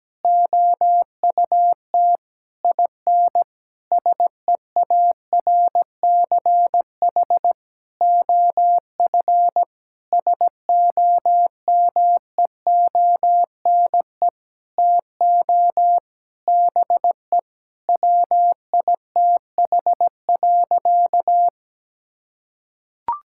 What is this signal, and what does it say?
Morse code, heard as a keyed tone, about 17 wpm, 700 Hz. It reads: OUT IN SEARCH OF SOMEONE TO BE WITH.
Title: Sentences from Top 1000 Words - Review Part 5 - 17wpm